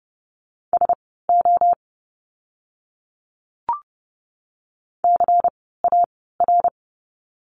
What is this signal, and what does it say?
Morse code, heard as a keyed tone, about 30 wpm, spaced as for 17 wpm, 700 Hz. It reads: SO CAR
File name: course / 30:10wpm / 108 Lesson 18f - Letter m - Words 30wpm